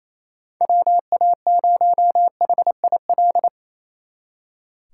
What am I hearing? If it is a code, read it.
Morse code, heard as a keyed tone, about 35 wpm, 700 Hz. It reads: WA0HIL
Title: Speed Racing - USA Call Signs 28wpm